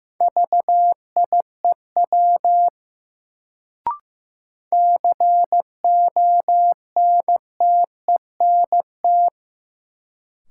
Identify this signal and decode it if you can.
Morse code, heard as a keyed tone, about 15 wpm, 700 Hz. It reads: VIEW CONTENT
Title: Sentences from Top 900 Words - New Word Drill - 15wpm